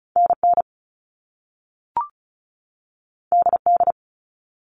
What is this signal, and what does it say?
Morse code, heard as a keyed tone, about 36 wpm, 700 Hz. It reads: NN DD
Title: All Cut Numbers - 2-Digit 35wpm